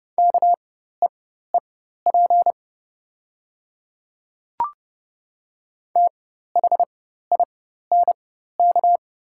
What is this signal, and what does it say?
Morse code, heard as a keyed tone, about 30 wpm, spaced as for 14 wpm, 700 Hz. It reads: KEEP THINK
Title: 4x Character Spacing - Top 300 Words - 30wpm